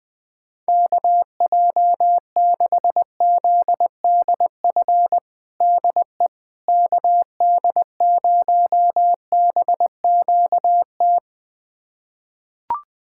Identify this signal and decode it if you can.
Morse code, heard as a keyed tone, about 21 wpm, 700 Hz. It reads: KJ6ZDF DE KD0BQT
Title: USA Callsign DE Callsign - 6 Characters - 20wpm